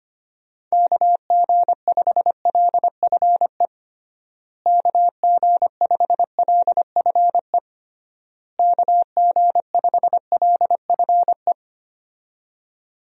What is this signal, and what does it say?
Morse code, heard as a keyed tone, about 25 wpm, 700 Hz. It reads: KG5LFE KG5LFE KG5LFE
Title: USA Call Signs - 6 Characters - Repeated 3x - 25wpm